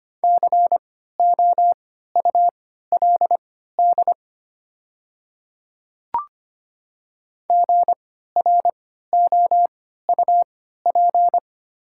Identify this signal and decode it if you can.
Morse code, heard as a keyed tone, about 25 wpm, spaced as for 14 wpm, 700 Hz. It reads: COULD GROUP
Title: 3x Character Spacing - Top 300 Words - 25wpm